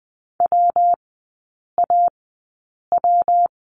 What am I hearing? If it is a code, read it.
Morse code, heard as a keyed tone, about 20 wpm, spaced as for 8 wpm, 700 Hz. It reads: WAW
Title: 145 Lesson 24e - Letter y - Three Characters 20wpm